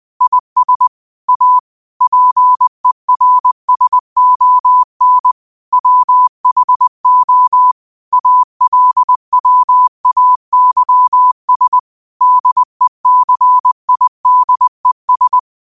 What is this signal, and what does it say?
Morse code, heard as a keyed tone, about 20 wpm, 1000 Hz. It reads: IS A PERSON WHO ALWAYS DECIDES
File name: qotdfiles / 20wpm